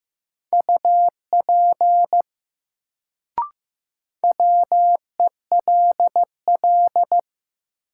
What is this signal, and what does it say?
Morse code, heard as a keyed tone, about 15 wpm, 700 Hz. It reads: UP WELL
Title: Sets of 2 Words - New Word Drill - 15wpm